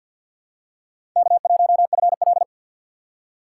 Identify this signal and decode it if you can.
Morse code, heard as a keyed tone, about 52 wpm, 700 Hz. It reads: K1FL
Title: USA Call Signs - 4 Characters - 6x Character Spacing - 50wpm